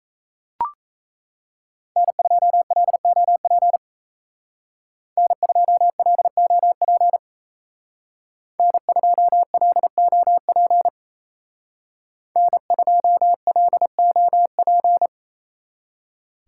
Morse code, heard as a keyed tone, about 33 wpm, 700 Hz. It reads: N2LOP N2LOP N2LOP N2LOP
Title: Speed Racing - USA Call Signs 28wpm